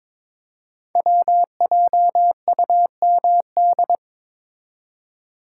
Morse code, heard as a keyed tone, about 22 wpm, 700 Hz. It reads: WJUMD
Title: Code Groups - 5 Letters - 22wpm